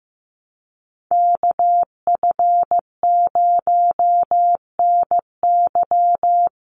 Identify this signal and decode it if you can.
Morse code, heard as a keyed tone, about 15 wpm, 700 Hz. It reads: KF0NY
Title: USA Call Signs 15wpm